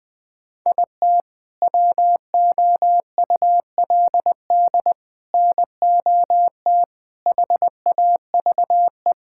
Morse code, heard as a keyed tone, about 20 wpm, 700 Hz. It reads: IT WOULD NOT HAVE
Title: Sentences with 7-Word Phrases - Spoken Beginning - 20wpm